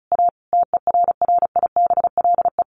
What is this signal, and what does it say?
Morse code, heard as a keyed tone, about 34 wpm, 700 Hz. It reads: A TERRIBLE
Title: A Sound Of Thunder 35wpm